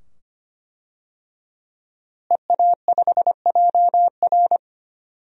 Morse code, heard as a keyed tone, about 25 wpm, 700 Hz. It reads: EA5JR
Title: DXCC-100 Call Signs - 5 Characters - Repeated 2x - 25wpm